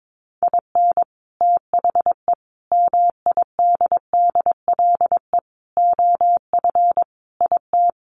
Morse code, heard as a keyed tone, about 22 wpm, 700 Hz. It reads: IN THE MIDDLE OF IT
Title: Sets of 5 Words Encore 22wpm